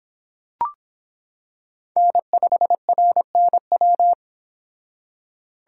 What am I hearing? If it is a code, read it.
Morse code, heard as a keyed tone, about 26 wpm, 700 Hz. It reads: N5RNW